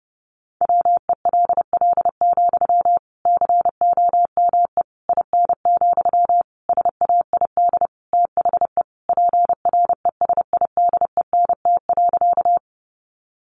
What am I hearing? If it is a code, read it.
Morse code, heard as a keyed tone, about 30 wpm, 700 Hz. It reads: WELL, COME IN, SAID THE PRESIDENT.
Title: Sentences from Top 1000 Words 30wpm